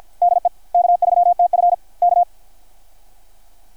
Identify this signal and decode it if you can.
Morse code, heard as a keyed tone, about 51 wpm, 690 Hz. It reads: DE K3TF K
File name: archived / CRSnet